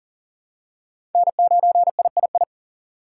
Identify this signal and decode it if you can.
Morse code, heard as a keyed tone, about 40 wpm, 700 Hz. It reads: N9III
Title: USA Call Signs - 6x Character Spacing - 40wpm